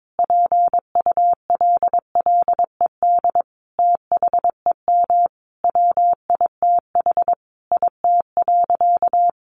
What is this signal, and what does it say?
Morse code, heard as a keyed tone, about 22 wpm, 700 Hz. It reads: PULLED THEM WITH IT.
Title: A Sound Of Thunder 22wpm